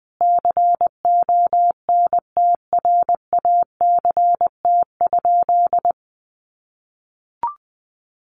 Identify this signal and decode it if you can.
Morse code, heard as a keyed tone, about 20 wpm, 700 Hz. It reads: CONTRACT?
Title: Sentences from Top 1400 Words 20wpm